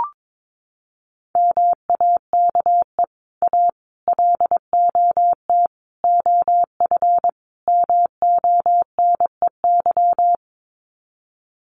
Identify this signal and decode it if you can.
Morse code, heard as a keyed tone, about 22 wpm, 700 Hz. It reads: MAKE A LOT OF MONEY
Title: Sets of 5 Words Encore2 22wpm